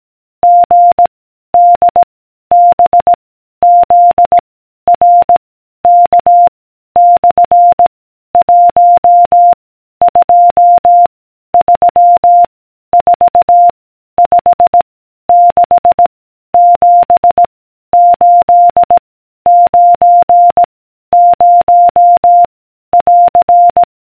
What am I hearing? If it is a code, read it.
Morse code, heard as a keyed tone, about 18 wpm, 700 Hz. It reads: G D B Z R K / 1 2 3 4 5 6 7 8 9 0 <AR>